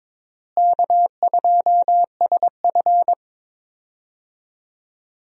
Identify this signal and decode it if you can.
Morse code, heard as a keyed tone, about 22 wpm, 700 Hz. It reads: K2SF
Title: USA Call Signs - 4 Characters - Repeated 3x - 22wpm